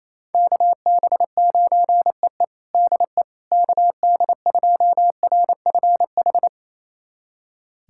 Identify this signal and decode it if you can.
Morse code, heard as a keyed tone, about 28 wpm, 700 Hz. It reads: KB9EE DE KD2RFH
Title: USA Callsign DE Callsign 28wpm